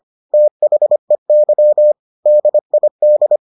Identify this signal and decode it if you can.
Morse code, heard as a keyed tone, about 25 wpm, 600 Hz. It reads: THEY DID